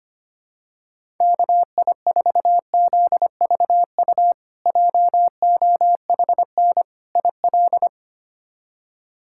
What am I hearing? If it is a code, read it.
Morse code, heard as a keyed tone, about 25 wpm, 700 Hz. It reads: KI4ZVU JOHN IL
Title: K1USN SST QSO Practice - 25wpm